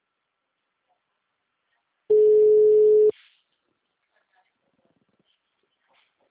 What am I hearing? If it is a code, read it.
Ringback tone (Europe, 425 Hz).
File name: out- -2029- - - .336